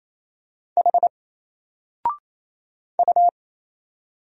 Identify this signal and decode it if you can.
Morse code, heard as a keyed tone, about 28 wpm, 700 Hz. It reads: H U
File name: Single-Letter-Number-28wpm